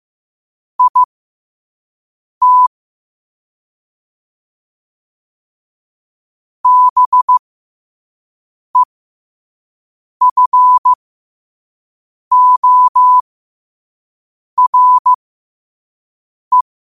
Morse code, heard as a keyed tone, about 15 wpm, spaced as for 5 wpm, 1000 Hz. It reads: IT BEFORE
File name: qotdfiles / 5wpm